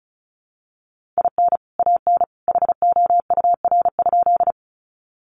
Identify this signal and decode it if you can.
Morse code, heard as a keyed tone, about 34 wpm, 700 Hz. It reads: IN AN HOUR?
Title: Sentences from Top 400 Words 35wpm